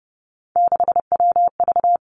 Morse code, heard as a keyed tone, about 30 wpm, 700 Hz. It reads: 6WV
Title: Three Challenging Letter-Number Combinations 30wpm